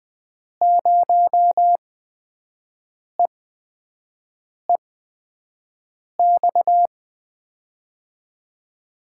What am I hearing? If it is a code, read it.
Morse code, heard as a keyed tone, about 20 wpm, spaced as for 5 wpm, 700 Hz. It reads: 0EEX
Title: USA Call Signs - 6 Characters - 8x Character Spacing - 20wpm